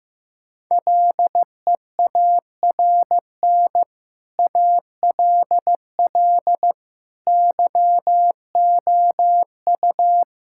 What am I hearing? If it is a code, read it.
Morse code, heard as a keyed tone, about 15 wpm, 700 Hz. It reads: LEARN ALL YOU
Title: Sentences from Top 1000 Words - Review Part 5 - 15wpm